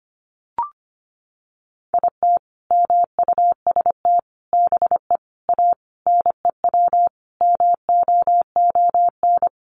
Morse code, heard as a keyed tone, about 25 wpm, 700 Hz. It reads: IT MUST BE A NEW MOON
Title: Sentences from Top 500 Words 25wpm